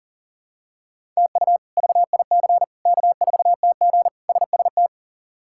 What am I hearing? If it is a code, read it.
Morse code, heard as a keyed tone, about 40 wpm, 700 Hz. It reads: TU VIC K4TG SST
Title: K1USN SST QSO Practice - 40wpm